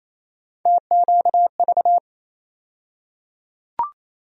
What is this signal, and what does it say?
Morse code, heard as a keyed tone, about 28 wpm, 700 Hz. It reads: TQV
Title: USA Call Sign Postfixes - 3 Letters 28wpm